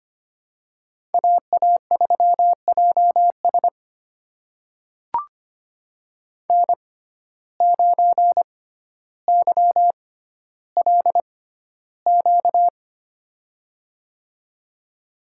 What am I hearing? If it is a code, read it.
Morse code, heard as a keyed tone, about 25 wpm, spaced as for 13 wpm, 700 Hz. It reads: AA3JS N9YLQ
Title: USA Call Signs - 5 Characters - 6x Character Spacing - 25wpm